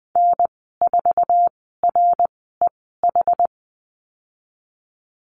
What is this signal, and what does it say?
Morse code, heard as a keyed tone, about 20 wpm, spaced as for 14 wpm, 700 Hz. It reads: N4REH
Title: 122 Lesson 20g - Number 3 - Call Signs 20wpm